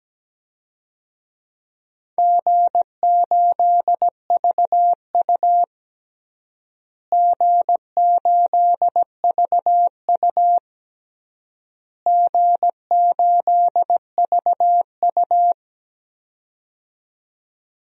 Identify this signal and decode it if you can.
Morse code, heard as a keyed tone, about 17 wpm, 700 Hz. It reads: G8VU G8VU G8VU
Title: DXCC-100 Call Signs - 4 Characters - Repeated 3x - 17wpm